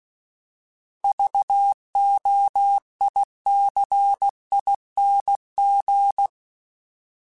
Morse code, heard as a keyed tone, about 16 wpm, 800 Hz. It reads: VOICING